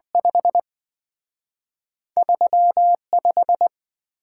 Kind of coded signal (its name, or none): Morse code